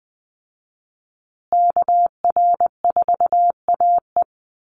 Morse code, heard as a keyed tone, about 20 wpm, 700 Hz. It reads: KR4AE
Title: Call Signs Repeated 2x - 20wpm